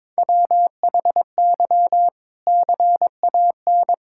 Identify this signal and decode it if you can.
Morse code, heard as a keyed tone, about 22 wpm, 700 Hz. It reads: WHY CAN
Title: Sentences from Top 1600 Words 22wpm